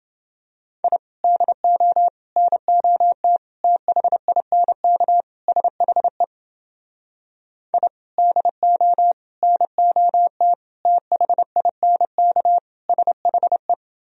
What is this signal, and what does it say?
Morse code, heard as a keyed tone, about 30 wpm, 700 Hz. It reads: I DO NOT THINK SHE I DO NOT THINK SHE